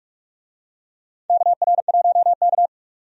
Morse code, heard as a keyed tone, about 46 wpm, 700 Hz. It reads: KR1K
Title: USA Call Signs - 4 Characters - 45wpm